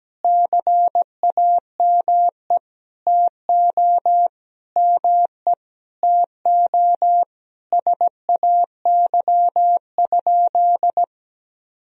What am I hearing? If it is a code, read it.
Morse code, heard as a keyed tone, about 17 wpm, 700 Hz. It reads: CAME TO ME TO SAY?
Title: Sentences from Top 200 Words and Abbreviations 17wpm